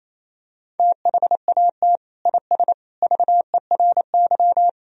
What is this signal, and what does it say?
Morse code, heard as a keyed tone, about 28 wpm, 700 Hz. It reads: THAT IS VERY
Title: Sentences from Top 1000 Words - Review Part 2 - 28wpm